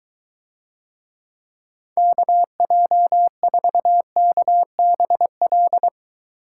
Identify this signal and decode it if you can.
Morse code, heard as a keyed tone, about 22 wpm, 700 Hz. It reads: KJ4KBL